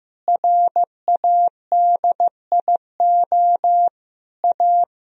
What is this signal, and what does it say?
Morse code, heard as a keyed tone, about 14 wpm, 700 Hz. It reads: RADIO A
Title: Sentences from Top 1000 Words - Review Part 3 - 15wpm